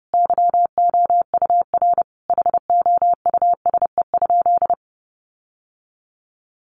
Morse code, heard as a keyed tone, about 30 wpm, 700 Hz. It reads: YOUR HOUSE?
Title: Sentences from Top 1500 Words 30wpm